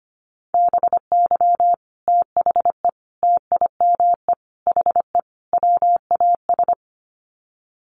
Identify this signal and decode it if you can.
Morse code, heard as a keyed tone, about 25 wpm, 700 Hz. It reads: BY THE TIME HE WAS
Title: Sets of 5 Words Encore 25wpm